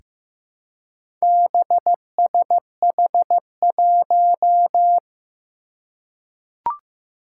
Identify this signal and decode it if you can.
Morse code, heard as a keyed tone, about 15 wpm, 700 Hz. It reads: BSH1
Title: Four Challenging Letter-Number Combinations 15wpm